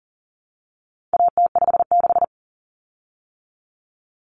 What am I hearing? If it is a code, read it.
Morse code, heard as a keyed tone, about 40 wpm, 700 Hz. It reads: AT56
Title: Common Cut Numbers - 4-Digit 40wpm